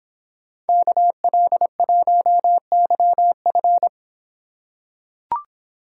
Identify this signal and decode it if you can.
Morse code, heard as a keyed tone, about 24 wpm, 700 Hz. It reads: KL1YF